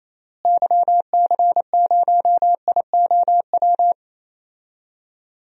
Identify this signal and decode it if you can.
Morse code, heard as a keyed tone, about 28 wpm, 700 Hz. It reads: YC0IOW
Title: DXCC-100 Call Signs - 6 Characters - Repeated 3x - 28wpm